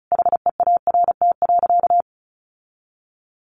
Morse code, heard as a keyed tone, about 34 wpm, 700 Hz. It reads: HEART.
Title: Sentences from Top 1400 Words 35wpm